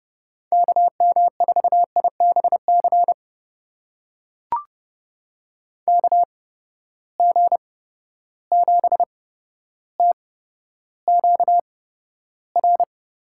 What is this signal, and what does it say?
Morse code, heard as a keyed tone, about 30 wpm, spaced as for 13 wpm, 700 Hz. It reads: KM4IBC KG7TQR
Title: USA Call Signs - 6 Characters - 8x Character Spacing - 30wpm